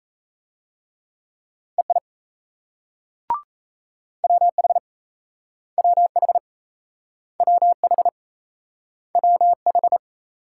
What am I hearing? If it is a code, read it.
Morse code, heard as a keyed tone, about 36 wpm, 700 Hz. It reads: EI WH WH WH WH